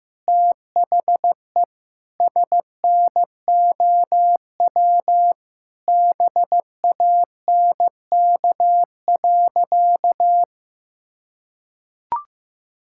Morse code, heard as a keyed tone, about 15 wpm, 700 Hz. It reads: THE SNOW BANK.
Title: Sentences from Top 1600 Words 15wpm